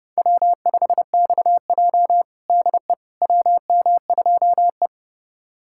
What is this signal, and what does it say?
Morse code, heard as a keyed tone, about 32 wpm, 700 Hz. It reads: W5XJ DE WM2E
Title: USA Callsign DE Callsign - 4 Characters - 30wpm